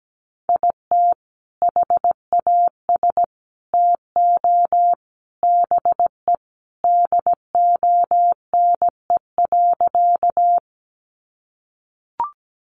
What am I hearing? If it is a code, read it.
Morse code, heard as a keyed tone, about 17 wpm, 700 Hz. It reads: IT HAS TO BE DONE.
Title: Sentences from Top 500 Words 17wpm